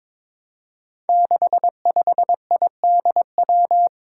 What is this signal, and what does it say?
Morse code, heard as a keyed tone, about 22 wpm, 700 Hz. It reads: 65IDW
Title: Five Challenging Letter-Number Combinations 22wpm